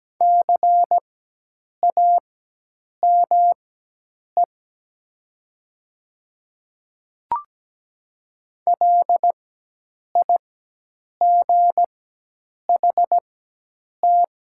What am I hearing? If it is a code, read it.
Morse code, heard as a keyed tone, about 17 wpm, spaced as for 8 wpm, 700 Hz. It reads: CAME LIGHT
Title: 4x Character Spacing - Top 300 Words - 17wpm